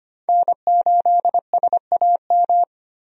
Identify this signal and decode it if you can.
Morse code, heard as a keyed tone, about 25 wpm, 690 Hz. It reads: N8SAM
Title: USA Call Signs - 5 Characters - 25wpm